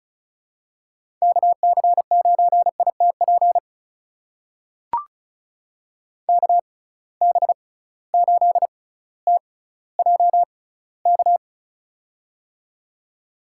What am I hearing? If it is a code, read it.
Morse code, heard as a keyed tone, about 36 wpm, spaced as for 18 wpm, 700 Hz. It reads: KC9ITP KB8TJK
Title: USA Call Signs - 6 Characters - 6x Character Spacing - 35wpm